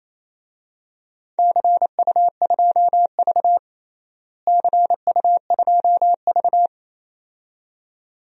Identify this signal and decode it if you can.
Morse code, heard as a keyed tone, about 27 wpm, 700 Hz. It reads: CU2V CU2V
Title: DXCC-100 Call Signs - 4 Characters - Repeated 2x - 28wpm